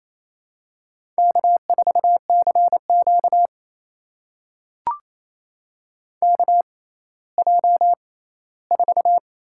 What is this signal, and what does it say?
Morse code, heard as a keyed tone, about 27 wpm, spaced as for 16 wpm, 700 Hz. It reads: K4CQ KJ4
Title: USA Call Signs - 6x Character Spacing - 28wpm